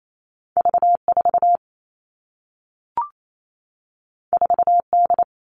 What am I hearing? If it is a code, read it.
Morse code, heard as a keyed tone, about 28 wpm, 700 Hz. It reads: V4 4D
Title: All Cut Numbers - 2-Digit 28wpm